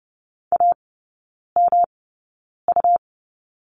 Morse code, heard as a keyed tone, about 30 wpm, spaced as for 9 wpm, 700 Hz. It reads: AMU
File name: Lesson 18e - Letter m - Three Characters 30wpm